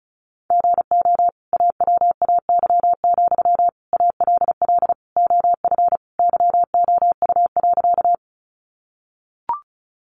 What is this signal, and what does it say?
Morse code, heard as a keyed tone, about 34 wpm, 700 Hz. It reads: GO AWAY, ALL OF YOU.
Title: Sentences from Top 600 Words 35wpm